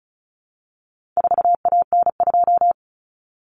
Morse code, heard as a keyed tone, about 34 wpm, 700 Hz. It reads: 4AN2
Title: Common Cut Numbers - 4-Digit 35wpm